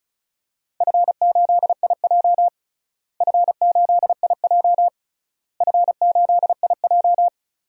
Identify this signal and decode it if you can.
Morse code, heard as a keyed tone, about 34 wpm, 700 Hz. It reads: F8IJ F8IJ F8IJ
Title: DXCC-100 Call Signs - 4 Characters - Repeated 3x - 35wpm